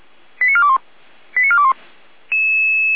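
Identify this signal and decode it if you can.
Five-tone selective call (ZVEI), heard as quick runs of five tones, tones of about 80 ms. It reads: 89421 89421
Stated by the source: ZVEI five-tone selective call received on an SDR.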